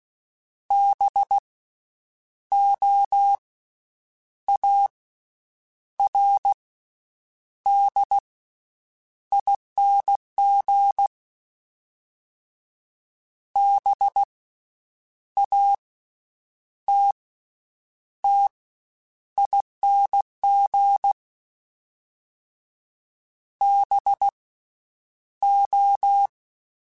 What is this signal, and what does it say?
Morse code, heard as a keyed tone, about 16 wpm, 800 Hz. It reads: B O A R D ING B A T T ING B O
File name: Special INGs-mp3s / Special INGs-mp3s / Slow ing @15